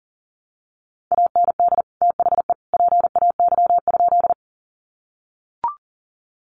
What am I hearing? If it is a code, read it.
Morse code, heard as a keyed tone, about 40 wpm, 700 Hz. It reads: AND THE PAY?
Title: Sentences from Top 700 Words 40wpm